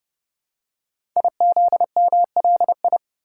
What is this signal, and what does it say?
Morse code, heard as a keyed tone, about 28 wpm, 690 Hz. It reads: IZMLI
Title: Code Groups - 5 Letters - 30wpm